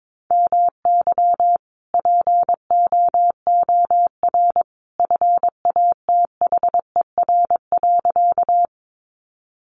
Morse code, heard as a keyed tone, about 22 wpm, 700 Hz. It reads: MY POOR FATHER.
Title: Sentences from Top 1000 Words 22wpm